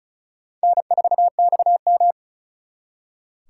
Morse code, heard as a keyed tone, about 45 wpm, 700 Hz. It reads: N4XM